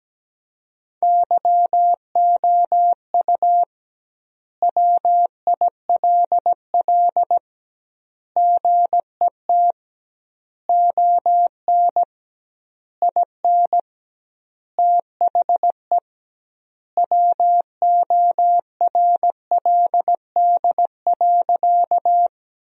Morse code, heard as a keyed tone, about 17 wpm, 700 Hz. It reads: YOU WILL GET ON IN THE WORLD.
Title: 2x Word-Spacing - Sentences from Top 200 Words - 17wpm